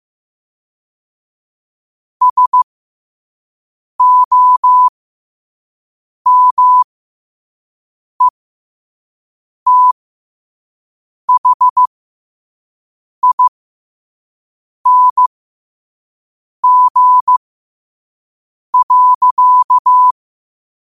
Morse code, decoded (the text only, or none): SOMETHING.